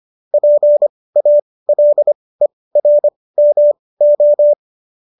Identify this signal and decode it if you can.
Morse code, heard as a keyed tone, about 25 wpm, spaced as for 18 wpm, 590 Hz. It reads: PALERMO
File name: qso215 18